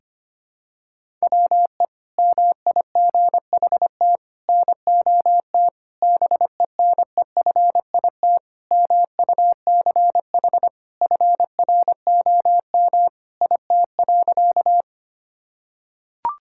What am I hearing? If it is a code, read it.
Morse code, heard as a keyed tone, about 25 wpm, 700 Hz. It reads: WE MIGHT NOT BENEFIT MUCH FROM IT.
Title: Sentences from Top 1000 Words - Review Part 6 - 25wpm